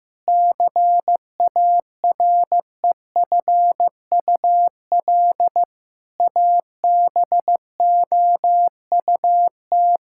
Morse code, heard as a keyed tone, about 15 wpm, 700 Hz. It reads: CAREFUL ABOUT
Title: Sentences from Top 1600 Words 15wpm